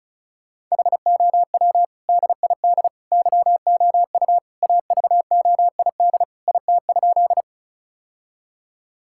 Morse code, heard as a keyed tone, about 36 wpm, 700 Hz. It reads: HOW DID YOU AVOID IT?
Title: Sentences from Top 1000 Words - Review Part 6 - 35wpm